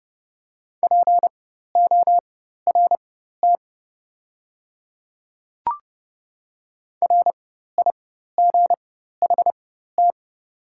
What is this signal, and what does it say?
Morse code, heard as a keyed tone, about 30 wpm, spaced as for 14 wpm, 700 Hz. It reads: PORT RIGHT
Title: 4x Character Spacing - Top 300 Words - 30wpm